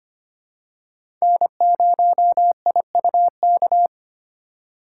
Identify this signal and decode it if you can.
Morse code, heard as a keyed tone, about 25 wpm, 700 Hz. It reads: N0IUK
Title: USA Call Signs - 4x Character Spacing - 25wpm